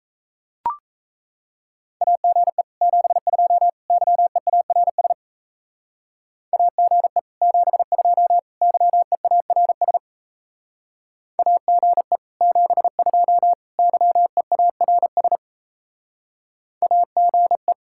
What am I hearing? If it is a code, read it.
Morse code, heard as a keyed tone, about 37 wpm, 700 Hz. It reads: AGE 72 YEARS AGE 72 YEARS AGE 72 YEARS AGE